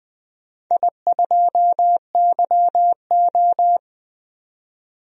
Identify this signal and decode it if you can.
Morse code, heard as a keyed tone, about 20 wpm, 700 Hz. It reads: I2YO
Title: DXCC-100 Call Signs - 4 Characters - 20wpm